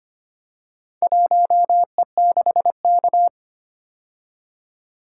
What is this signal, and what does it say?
Morse code, heard as a keyed tone, about 25 wpm, 700 Hz. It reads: 1E6K
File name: practice / 25wpm / Four Letter-Number Combinations - 25wpm